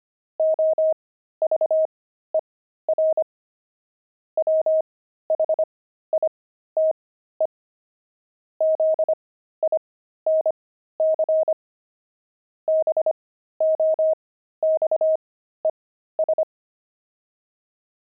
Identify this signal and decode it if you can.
Morse code, heard as a keyed tone, about 25 wpm, spaced as for 13 wpm, 630 Hz. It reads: OVER WHITE ZINC BOXES
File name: pangram-fs-13-cs-25-zinc